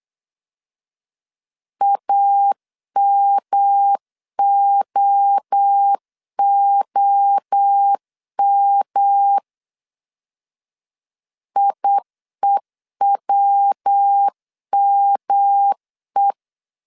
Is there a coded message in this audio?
Yes: Morse code from 1.8 to 16.3 s, heard as a keyed tone, about 8 wpm, 800 Hz. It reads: AMOOM IEWME